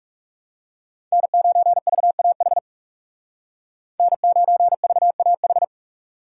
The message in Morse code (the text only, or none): N9VAH N9VAH